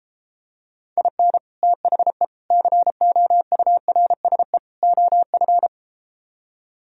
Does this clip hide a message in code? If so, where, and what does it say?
Yes: Morse code from 1.0 to 5.7 s, heard as a keyed tone, about 33 wpm, 700 Hz. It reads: IN THE COURSE OF